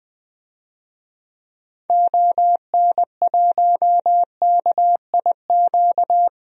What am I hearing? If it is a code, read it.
Morse code, heard as a keyed tone, about 21 wpm, 700 Hz. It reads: ON1KIQ
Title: DXCC-100 Call Signs - 6 Characters - Repeated 2x - 20wpm